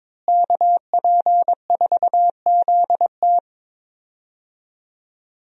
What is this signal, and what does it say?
Morse code, heard as a keyed tone, about 22 wpm, 700 Hz. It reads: KP4ZT
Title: USA Call Signs - 5 Characters - Repeated 2x - 22wpm